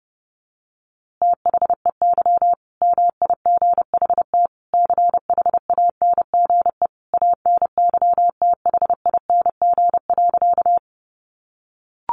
Morse code, heard as a keyed tone, about 30 wpm, 700 Hz. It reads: THEY MIGHT CHANGE ANYTHING.
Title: A Sound Of Thunder 30wpm